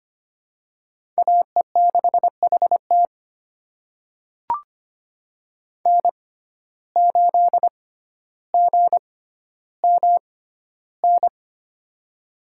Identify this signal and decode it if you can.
Morse code, heard as a keyed tone, about 25 wpm, spaced as for 13 wpm, 700 Hz. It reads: AE6HT N8GMN
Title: USA Call Signs - 5 Characters - 6x Character Spacing - 25wpm